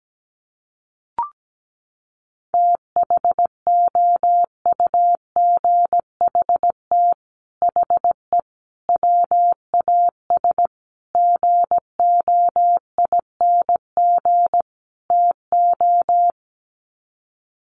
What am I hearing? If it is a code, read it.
Morse code, heard as a keyed tone, about 17 wpm, 700 Hz. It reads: THOUGHT HE WAS GOING TO
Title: Sets of 5 Words Encore2 17wpm